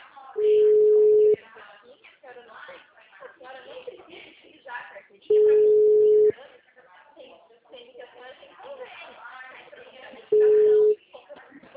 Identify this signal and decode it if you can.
Ringback tone (Europe, 425 Hz), with speech.